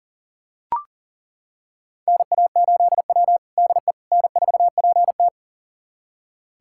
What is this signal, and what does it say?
Morse code, heard as a keyed tone, about 37 wpm, 700 Hz. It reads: NA8W DE N4PT
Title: USA Callsign DE Callsign - 4 Characters - 40wpm